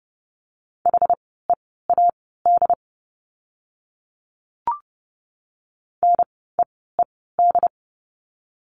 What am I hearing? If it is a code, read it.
Morse code, heard as a keyed tone, about 30 wpm, spaced as for 17 wpm, 700 Hz. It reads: HEAD NEED
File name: Lesson 17f - Letter u - Words 30wpm